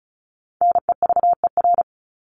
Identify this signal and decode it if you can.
Morse code, heard as a keyed tone, about 34 wpm, 700 Hz. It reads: NEVER
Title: Top 500 Words 35wpm